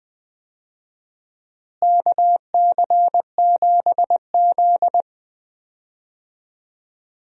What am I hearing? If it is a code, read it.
Morse code, heard as a keyed tone, about 21 wpm, 700 Hz. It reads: KC7Z
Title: USA Call Signs - 4 Characters - Repeated 2x - 20wpm